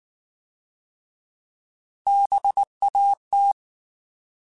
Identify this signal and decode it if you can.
Morse code, heard as a keyed tone, about 19 wpm, 800 Hz. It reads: BAT
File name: Alphabetical / Alphabetical mp3s / Bs @18